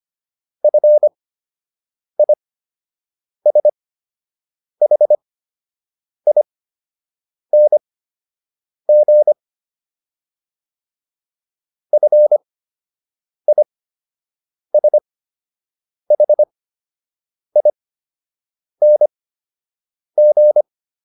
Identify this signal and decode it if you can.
Morse code, heard as a keyed tone, about 25 wpm, spaced as for 7 wpm, 600 Hz. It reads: FISHING FISHING